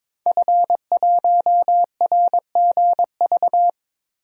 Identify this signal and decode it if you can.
Morse code, heard as a keyed tone, about 22 wpm, 700 Hz. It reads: F1RGV
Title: DXCC-100 Call Signs - 5 Characters - 22wpm